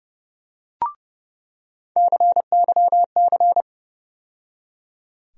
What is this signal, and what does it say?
Morse code, heard as a keyed tone, about 30 wpm, 700 Hz. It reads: CYC
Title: USA Call Sign Postfixes - 3 Letters 30wpm